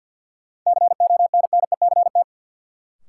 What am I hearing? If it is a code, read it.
Morse code, heard as a keyed tone, about 50 wpm, 700 Hz. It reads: CONNECT
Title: Sentences from Top 1500 Words - New Word Drill Alt - 50wpm